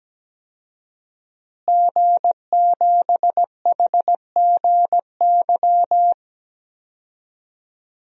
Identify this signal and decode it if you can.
Morse code, heard as a keyed tone, about 17 wpm, 700 Hz. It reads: G7HGY
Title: DXCC-100 Call Signs - 5 Characters - Repeated 2x - 17wpm